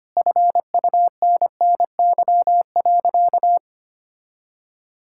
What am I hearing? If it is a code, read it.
Morse code, heard as a keyed tone, about 25 wpm, 700 Hz. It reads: FUNNY.
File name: practice / 25wpm / All-Age Appropriate Jokes - 25wpm